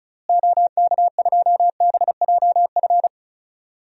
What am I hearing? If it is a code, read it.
Morse code, heard as a keyed tone, about 36 wpm, 700 Hz. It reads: OK2BJF
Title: DXCC-100 Call Signs - 6 Characters - Repeated 3x - 35wpm